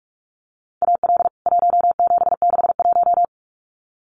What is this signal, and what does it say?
Morse code, heard as a keyed tone, about 44 wpm, 700 Hz. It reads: AL 1761
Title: CWT Exchange 45wpm